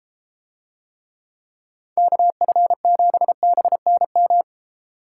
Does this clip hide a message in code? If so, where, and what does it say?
Yes: Morse code from 2.0 to 4.4 s, heard as a keyed tone, about 32 wpm, 700 Hz. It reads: KF7BNM